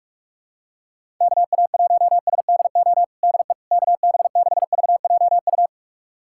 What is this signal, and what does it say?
Morse code, heard as a keyed tone, about 41 wpm, 700 Hz. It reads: KA1SDO DE KB6VJU
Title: USA Callsign DE Callsign - 6 Characters - 45wpm